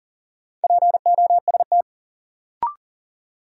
Morse code, heard as a keyed tone, about 40 wpm, 700 Hz. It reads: POST
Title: Sentences from Top 1000 Words - New Word Drill - 40wpm